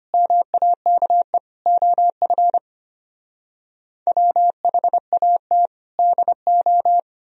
Morse code, heard as a keyed tone, about 33 wpm, 700 Hz. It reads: MAKE OF WHAT DO